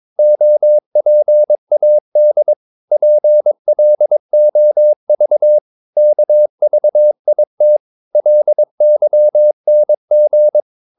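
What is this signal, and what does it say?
Morse code, heard as a keyed tone, about 22 wpm, 600 Hz. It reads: OPAD PLOV KVIT LYNG
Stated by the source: Morse practice of short Danish words at 22 wpm, from CW med Gnister.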